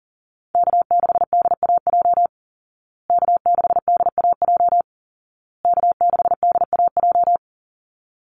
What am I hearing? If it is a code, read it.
Morse code, heard as a keyed tone, about 40 wpm, 700 Hz. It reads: K6DAJ K6DAJ K6DAJ
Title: Call Signs Repeated 2x - 40wpm